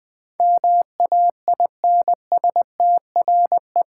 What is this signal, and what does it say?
Morse code, heard as a keyed tone, about 20 wpm, 700 Hz. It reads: MAINSTRE
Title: Build A Word - 4 Letters At A Time - 10-Letter Words - 20wpm